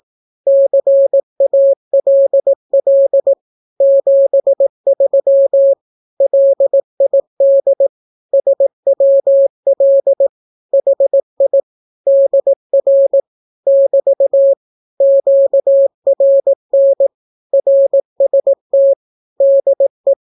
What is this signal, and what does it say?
Morse code, heard as a keyed tone, about 18 wpm, 550 Hz. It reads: CALL 73 LID SWL HI DR <BT> QRN RST DE
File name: naukapodstaw / skroty / skroty